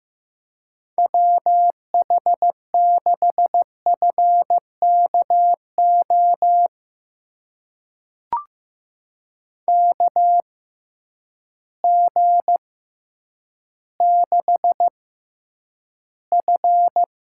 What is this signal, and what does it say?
Morse code, heard as a keyed tone, about 15 wpm, spaced as for 9 wpm, 700 Hz. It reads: WH6FKO KG6F
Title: USA Call Signs - 6 Characters - 6x Character Spacing - 15wpm